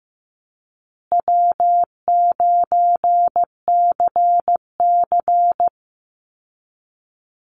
Morse code, heard as a keyed tone, about 15 wpm, 700 Hz. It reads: W9CC
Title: Easy Call Signs Repeated 2x - 15wpm